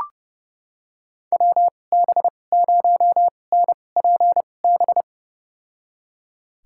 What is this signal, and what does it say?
Morse code, heard as a keyed tone, about 29 wpm, spaced as for 21 wpm, 700 Hz. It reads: WB0NPB
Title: USA Call Signs - 2x Character Spacing - 30wpm